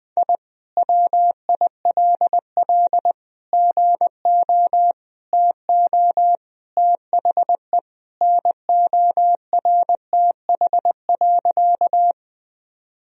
Morse code, heard as a keyed tone, about 21 wpm, 700 Hz. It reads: I WILL GO TO THE NORTH.
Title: Sentences from Top 1000 Words - Review Part 1 - 20wpm